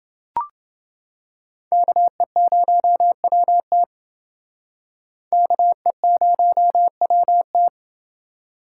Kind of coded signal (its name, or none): Morse code